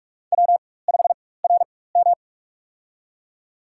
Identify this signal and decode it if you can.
Morse code, heard as a keyed tone, about 46 wpm, spaced as for 21 wpm, 700 Hz. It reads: W5RM
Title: USA Call Signs - 4 Characters - 4x Character Spacing - 45wpm